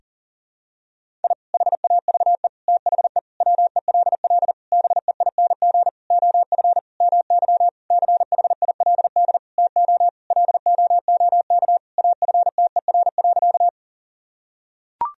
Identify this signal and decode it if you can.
Morse code, heard as a keyed tone, about 40 wpm, 700 Hz. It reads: I HAVE THE WELL BEING OF MY CHILD TO LOOK AFTER.
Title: Sentences from Top 1000 Words - Review Part 3 - 40wpm